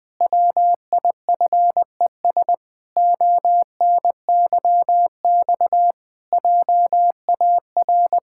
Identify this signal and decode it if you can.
Morse code, heard as a keyed tone, about 20 wpm, 700 Hz. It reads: WIFES ONYX JAR
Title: Pangram Challenge v2 - 20wpm